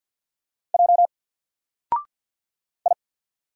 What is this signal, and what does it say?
Morse code, heard as a keyed tone, about 50 wpm, 700 Hz. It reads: J I
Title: Single Letter-Number 50wpm